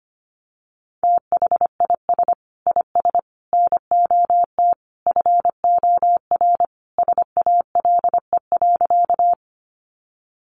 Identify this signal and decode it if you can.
Morse code, heard as a keyed tone, about 25 wpm, 700 Hz. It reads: THIS IS NOT FOR SALE.
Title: Sentences from Top 1400 Words 25wpm